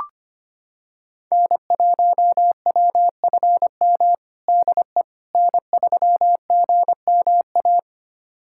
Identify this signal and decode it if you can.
Morse code, heard as a keyed tone, about 25 wpm, 700 Hz. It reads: N1WFM DE N3GMA
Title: USA Callsign DE Callsign - 5 Characters - 25wpm